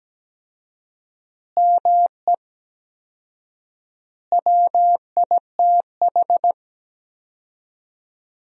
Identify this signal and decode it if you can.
Morse code, heard as a keyed tone, about 17 wpm, 700 Hz. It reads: ME WITH
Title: 4x Word-Spacing - Sentences from Top 200 Words - 17wpm